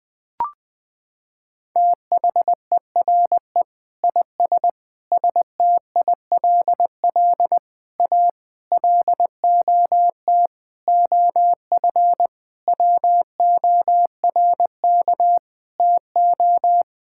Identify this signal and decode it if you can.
Morse code, heard as a keyed tone, about 20 wpm, 700 Hz. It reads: THERE IS STILL A LOT OF WORK TO
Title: Sets of 9 Words - 20wpm